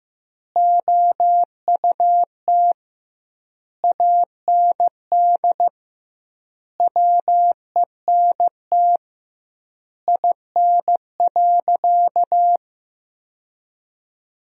Morse code, heard as a keyed tone, about 15 wpm, 700 Hz. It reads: OUT AND WENT IN.
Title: 2x Word-Spacing - Sentences from Top 300 Words - 15wpm